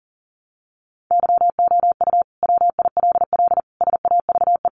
Morse code, heard as a keyed tone, about 40 wpm, 700 Hz. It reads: YOU WILL SAVE
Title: Sentences from Top 1100 Words 40wpm